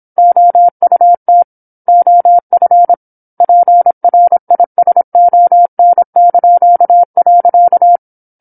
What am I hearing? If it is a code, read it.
Morse code, heard as a keyed tone, about 27 wpm, 700 Hz. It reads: OUT OF PRISON).